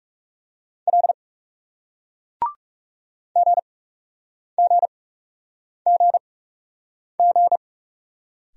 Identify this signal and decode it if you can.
Morse code, heard as a keyed tone, about 41 wpm, spaced as for 8 wpm, 700 Hz. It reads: L GGGG